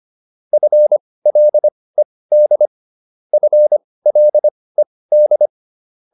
Morse code, heard as a keyed tone, about 25 wpm, spaced as for 18 wpm, 600 Hz. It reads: FLED FLED